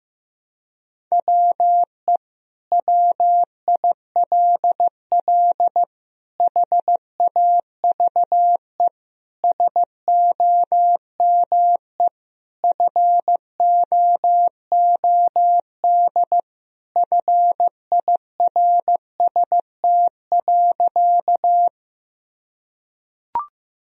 Morse code, heard as a keyed tone, about 15 wpm, 700 Hz. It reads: WE WILL HAVE SOME FOOD FIRST.
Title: Sentences from Top 1000 Words - Review Part 1 - 15wpm